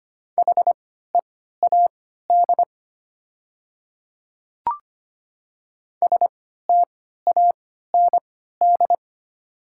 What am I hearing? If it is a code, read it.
Morse code, heard as a keyed tone, about 25 wpm, spaced as for 14 wpm, 690 Hz. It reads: HEAD STAND
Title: 3x Character Spacing - Top 500 Words - 25wpm